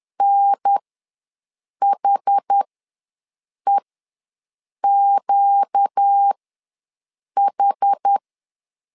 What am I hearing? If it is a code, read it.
Morse code, heard as a keyed tone, about 10 wpm, spaced as for 6 wpm, 800 Hz. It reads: NHEQH
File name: LEZIONI / 4 Q ws 10wpm 2306